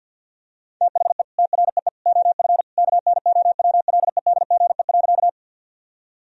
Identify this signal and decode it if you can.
Morse code, heard as a keyed tone, about 50 wpm, 700 Hz. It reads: THE TREE OF KNOWLEDGE.
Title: Sentences from Top 1000 Words - Review Part 3 - 50wpm